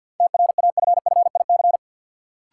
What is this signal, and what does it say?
Morse code, heard as a keyed tone, about 51 wpm, 700 Hz. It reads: TRAFFIC
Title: Sentences from Top 1300 Words - New Word Drill Alt - 50wpm